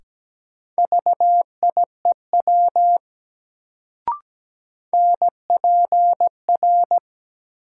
Morse code, heard as a keyed tone, about 17 wpm, 700 Hz. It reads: VIEW NPR
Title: Sets of 5 Words - New Word Drill - 17wpm